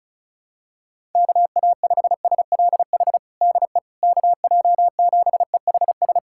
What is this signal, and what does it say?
Morse code, heard as a keyed tone, about 36 wpm, 700 Hz. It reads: KA5SLH DE KJ7EHS
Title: USA Callsign DE Callsign 35wpm